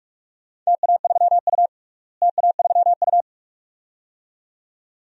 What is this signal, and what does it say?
Morse code, heard as a keyed tone, about 48 wpm, 700 Hz. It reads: TA3U TA3U
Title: DXCC-100 Call Signs - 4 Characters - Repeated 2x - 45wpm